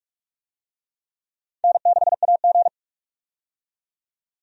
Morse code, heard as a keyed tone, about 46 wpm, 700 Hz. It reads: N6AG
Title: USA Call Signs - 4 Characters - 45wpm